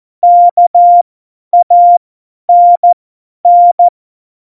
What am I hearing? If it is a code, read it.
Morse code, heard as a keyed tone, about 14 wpm, spaced as for 10 wpm, 700 Hz. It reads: KANN
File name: Morsekurs / Witze / Witz 25. Dezember, 50 F70